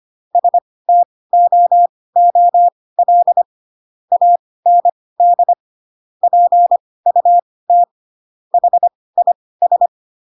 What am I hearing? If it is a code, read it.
Morse code, heard as a keyed tone, about 25 wpm, spaced as for 18 wpm, 700 Hz. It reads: STOOL AND PUT HIS
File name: Pinocchio / Pinocchio ch6 12 25 ik1hge